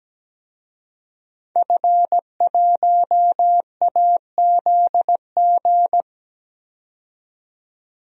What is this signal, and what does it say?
Morse code, heard as a keyed tone, about 17 wpm, 700 Hz. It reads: F1AZG
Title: DXCC-100 Call Signs - 5 Characters - 17wpm